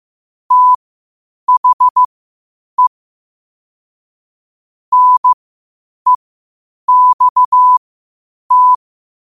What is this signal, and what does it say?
Morse code, heard as a keyed tone, about 15 wpm, spaced as for 8 wpm, 1000 Hz. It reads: THE NEXT
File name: qotdfiles / 8wpm